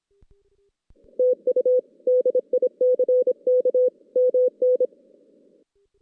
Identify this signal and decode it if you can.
Morse code, heard as a keyed tone, about 26 wpm, 500 Hz. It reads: TU DICK MN
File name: 2016 JAN CW NAQP / 2016 Jan naqp k5ot - 0-5